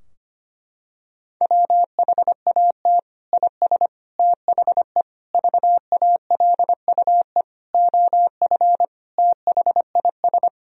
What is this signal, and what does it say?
Morse code, heard as a keyed tone, about 25 wpm, 700 Hz. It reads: WHAT IS THE VALUE OF THIS
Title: Sentences from Top 1000 Words - Review Part 3 - 25wpm